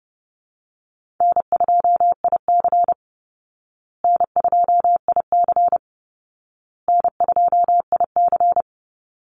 Morse code, heard as a keyed tone, about 30 wpm, 700 Hz. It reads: N2IC N2IC N2IC
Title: Call Signs Repeated 2x - 30wpm